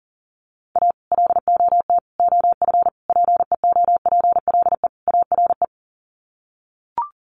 Morse code, heard as a keyed tone, about 40 wpm, 700 Hz. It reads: A LOT OF PEOPLE ARE
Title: Sets of 5 Words 40wpm